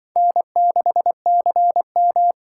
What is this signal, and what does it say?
Morse code, heard as a keyed tone, about 24 wpm, 700 Hz. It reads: N6CM
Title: Speed Racing - USA 1x2 Call Signs 20wpm